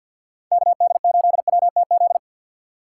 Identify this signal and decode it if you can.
Morse code, heard as a keyed tone, about 50 wpm, 700 Hz. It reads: KD8WTZ
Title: USA Call Signs - 6 Characters - 6x Character Spacing - 50wpm